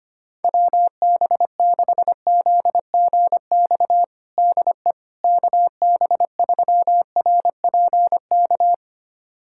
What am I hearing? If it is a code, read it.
Morse code, heard as a keyed tone, about 25 wpm, 700 Hz. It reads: WB6ZGX DE KB3RPK
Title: USA Callsign DE Callsign 25wpm